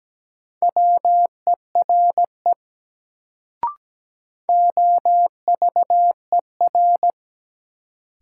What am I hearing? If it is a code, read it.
Morse code, heard as a keyed tone, about 17 wpm, 700 Hz. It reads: WERE OVER
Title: Sentences from Top 100 Words - New Word Drill - 17wpm